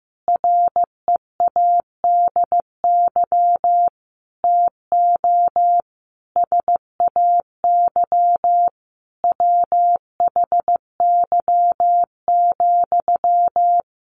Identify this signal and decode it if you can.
Morse code, heard as a keyed tone, about 15 wpm, 700 Hz. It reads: READY TO SAY WHY,
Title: Sentences from Top 1300 Words 15wpm